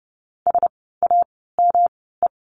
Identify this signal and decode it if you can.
Morse code, heard as a keyed tone, about 30 wpm, spaced as for 17 wpm, 700 Hz. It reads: SAME
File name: course / 30:10wpm / 230 Lesson 37f - Letter x - Words 30wpm